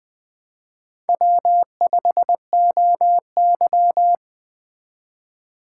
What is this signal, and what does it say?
Morse code, heard as a keyed tone, about 20 wpm, 700 Hz. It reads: W5OY
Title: Speed Racing - USA 1x2 Call Signs 17wpm